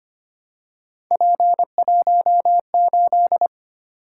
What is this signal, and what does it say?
Morse code, heard as a keyed tone, about 25 wpm, 700 Hz. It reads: P18
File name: Three Letter-Number Combinations - 25wpm